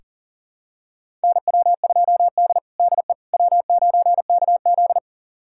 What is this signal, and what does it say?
Morse code, heard as a keyed tone, about 40 wpm, 700 Hz. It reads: NW2D DE W9KZ
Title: USA Callsign DE Callsign - 4 Characters - 40wpm